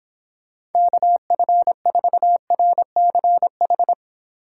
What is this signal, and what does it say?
Morse code, heard as a keyed tone, about 31 wpm, 700 Hz. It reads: KF4RCH